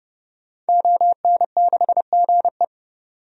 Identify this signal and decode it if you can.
Morse code, heard as a keyed tone, about 30 wpm, 700 Hz. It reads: ON6GE
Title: DXCC-100 Call Signs - 5 Characters - 30wpm